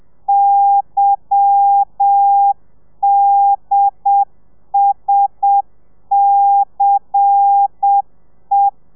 Morse code, heard as a keyed tone, about 7 wpm, 800 Hz. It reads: YDSCE
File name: telegrafia / i2tfj / Lez10 30